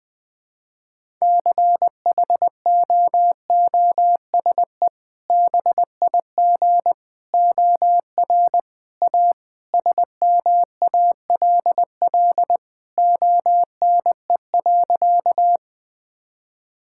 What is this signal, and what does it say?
Morse code, heard as a keyed tone, about 20 wpm, 700 Hz. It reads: CHOOSE BIG OR A SMALL ONE.
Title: Sentences with 2-Word Phrases - 20wpm